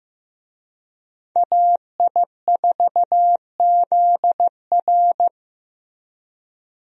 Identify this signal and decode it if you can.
Morse code, heard as a keyed tone, about 16 wpm, 700 Hz. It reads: AI4ZR
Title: USA Call Signs - 5 Characters - Repeated 3x - 15wpm